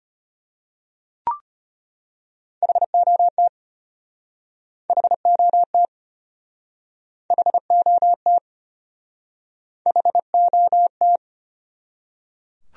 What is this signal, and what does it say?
Morse code, heard as a keyed tone, about 34 wpm, 700 Hz. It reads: HOT HOT HOT HOT